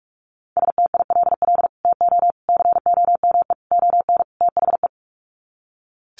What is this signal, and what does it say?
Morse code, heard as a keyed tone, about 44 wpm, 700 Hz. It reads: STILL TO COME ON THE
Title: Sets of 5 Words Encore2 45wpm